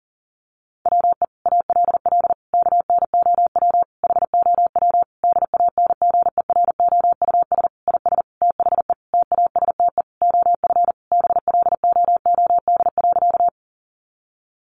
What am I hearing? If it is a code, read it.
Morse code, heard as a keyed tone, about 37 wpm, 700 Hz. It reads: WE ALL KNOW HOW DANGEROUS IS THE TASTE OF BLOOD.
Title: Sentences from Top 1200 Words 40wpm